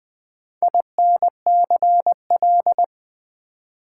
Morse code, heard as a keyed tone, about 20 wpm, 700 Hz. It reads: INCL
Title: Build A Word - 2 Letters At A Time - 8-Letter Words - 20wpm